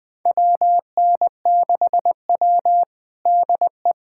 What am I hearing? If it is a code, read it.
Morse code, heard as a keyed tone, about 20 wpm, 700 Hz. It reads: WN6W DE
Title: USA Callsign DE Callsign - 4 Characters - 20wpm